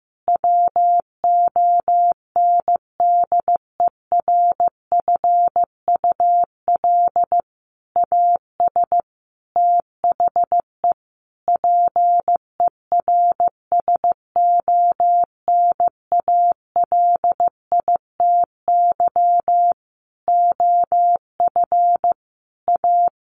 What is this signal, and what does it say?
Morse code, heard as a keyed tone, about 15 wpm, 700 Hz. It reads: WONDERFUL AS THE PERSONALITY OF A
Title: Sentences from Top 1400 Words 15wpm